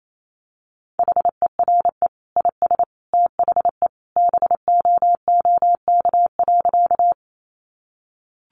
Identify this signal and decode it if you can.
Morse code, heard as a keyed tone, about 28 wpm, 700 Hz. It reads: HERE IS THE BOOK.
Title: Sentences from Top 300 Words 28wpm